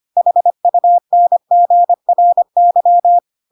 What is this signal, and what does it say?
Morse code, heard as a keyed tone, about 25 wpm, 700 Hz. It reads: HUNGRY
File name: Pinocchio / Pinocchio ch5 25 25 ik1hge